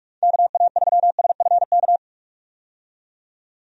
Morse code, heard as a keyed tone, about 46 wpm, 700 Hz. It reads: KA3SFK
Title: USA Call Signs - 6 Characters - 45wpm